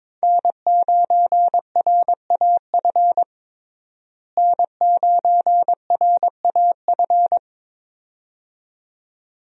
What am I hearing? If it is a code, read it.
Morse code, heard as a keyed tone, about 22 wpm, 700 Hz. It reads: N9RAF N9RAF
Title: USA Call Signs - 5 Characters - Repeated 2x - 22wpm